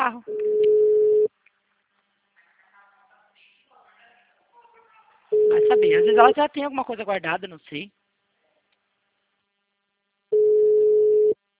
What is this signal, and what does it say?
Ringback tone (Europe, 425 Hz), with speech.